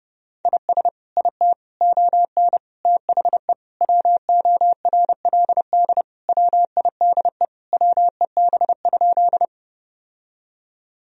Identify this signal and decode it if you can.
Morse code, heard as a keyed tone, about 30 wpm, 700 Hz. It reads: IS IT ON THE WORLD WIDE WEB?
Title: Sentences from Top 1000 Words - Review Part 4 - 30wpm